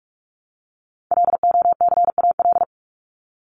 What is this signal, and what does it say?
Morse code, heard as a keyed tone, about 46 wpm, 700 Hz. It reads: LOCAL
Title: Top 501-750 Words 45wpm